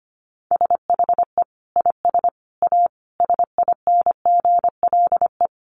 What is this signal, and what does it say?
Morse code, heard as a keyed tone, about 25 wpm, 700 Hz. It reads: SHE IS A SINGLE
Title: Sentences from Top 700 Words 25wpm